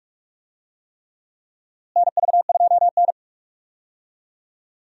Morse code, heard as a keyed tone, about 44 wpm, 700 Hz. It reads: NV2N
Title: USA Call Signs - 4 Characters - Repeated 3x - 45wpm